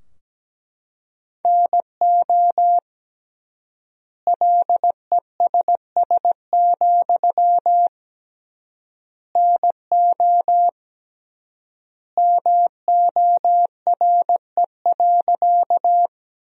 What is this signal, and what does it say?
Morse code, heard as a keyed tone, about 17 wpm, 700 Hz. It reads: NO LESS, NO MORE.
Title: 3x Word-Spacing - Sentences from Top 400 Words - 17wpm